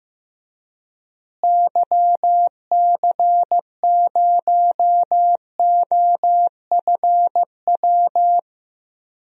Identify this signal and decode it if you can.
Morse code, heard as a keyed tone, about 15 wpm, 700 Hz. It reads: YC0OFW
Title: DXCC-100 Call Signs - 6 Characters - Repeated 3x - 15wpm